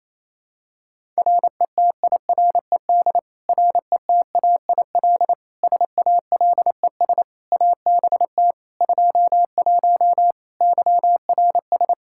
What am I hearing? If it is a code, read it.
Morse code, heard as a keyed tone, about 28 wpm, 700 Hz. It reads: RETIRED RETAIL SALES ABT 21 YRS
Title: Practice QSO - Exchange 3 - More Info - 28wpm